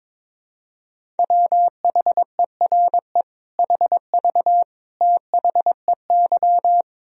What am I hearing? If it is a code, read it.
Morse code, heard as a keyed tone, about 22 wpm, 700 Hz. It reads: WHERE HV THEY
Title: Sentences from Top 200 Words and Abbreviations 22wpm